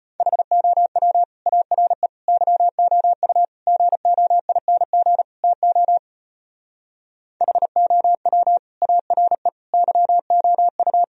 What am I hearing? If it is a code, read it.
Morse code, heard as a keyed tone, about 33 wpm, 700 Hz. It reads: HOW ARE YOU GOING TO HOW ARE YOU